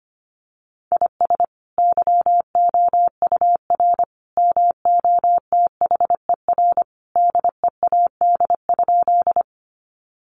Morse code, heard as a keyed tone, about 25 wpm, 700 Hz. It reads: IS YOUR MOTHER DEAD?
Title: Sentences from Top 1500 Words 25wpm